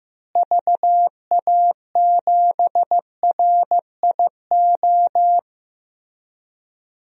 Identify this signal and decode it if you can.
Morse code, heard as a keyed tone, about 16 wpm, 700 Hz. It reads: VA7RIO
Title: DXCC-100 Call Signs - 6 Characters - Repeated 2x - 15wpm